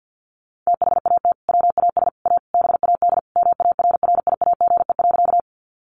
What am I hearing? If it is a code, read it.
Morse code, heard as a keyed tone, about 50 wpm, 700 Hz. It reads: THAT WAS A BAD MARRIAGE.
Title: Sentences from Top 800 Words 50wpm